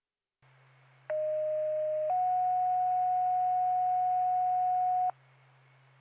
Two-tone page, heard as a one-second tone, then a three-second tone, about 620 then 730 Hz. It is heard from 1.1 to 5.1 s.